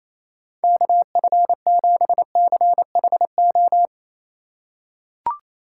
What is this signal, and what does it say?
Morse code, heard as a keyed tone, about 28 wpm, 700 Hz. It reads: KF7CHO
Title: USA Call Signs - 6 Characters - 4x Character Spacing - 28wpm